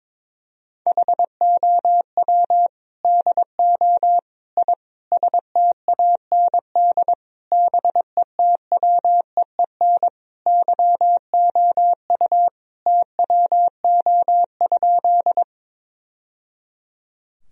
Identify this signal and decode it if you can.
Morse code, heard as a keyed tone, about 22 wpm, 700 Hz. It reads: HOW DO I STAND BETWEEN YOU TWO?
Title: Sentences from Top 1000 Words - Review Part 1 - 22wpm